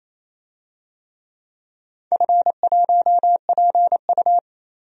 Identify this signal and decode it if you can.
Morse code, heard as a keyed tone, about 27 wpm, 690 Hz. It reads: F1PU
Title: DXCC-100 Call Signs - 4 Characters - 28wpm